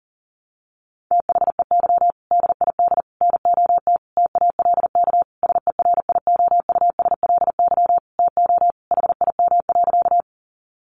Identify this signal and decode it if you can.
Morse code, heard as a keyed tone, about 40 wpm, 700 Hz. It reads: THEY DID NOT TALK SERIOUSLY TO HIM.
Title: Sentences from Top 1200 Words 40wpm